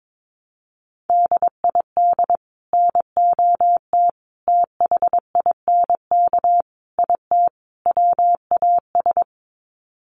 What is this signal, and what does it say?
Morse code, heard as a keyed tone, about 22 wpm, 700 Hz. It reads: DID NOT THINK IT WAS
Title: Sets of 5 Words Encore 22wpm